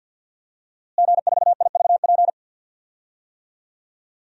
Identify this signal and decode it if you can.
Morse code, heard as a keyed tone, about 53 wpm, 700 Hz. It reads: G4IVP
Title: DXCC-100 Call Signs - 5 Characters - Repeated 3x - 50wpm